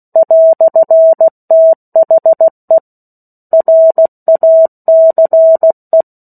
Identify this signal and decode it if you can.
Morse code, heard as a keyed tone, about 16 wpm, 650 Hz. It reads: "THE RACE
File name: 03-Chapter-03 16 WPM